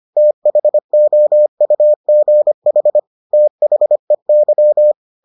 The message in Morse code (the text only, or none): THOUGH THEY